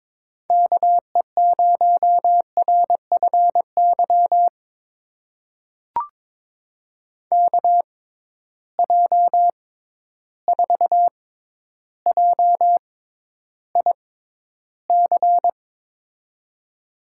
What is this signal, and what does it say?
Morse code, heard as a keyed tone, about 22 wpm, spaced as for 11 wpm, 700 Hz. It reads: KE0RFY KJ4JIC
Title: USA Call Signs - 6 Characters - 6x Character Spacing - 22wpm